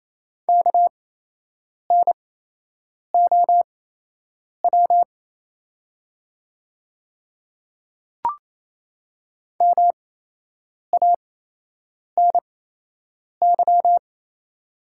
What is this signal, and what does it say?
Morse code, heard as a keyed tone, about 28 wpm, spaced as for 8 wpm, 700 Hz. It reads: KNOW MANY
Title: 8x Character Spacing - Top 100 Words - 28wpm